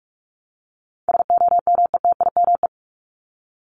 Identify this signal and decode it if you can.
Morse code, heard as a keyed tone, about 44 wpm, 700 Hz. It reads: SOMETIME
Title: ES:S Suffix - Top 100 Words with Base Words 45wpm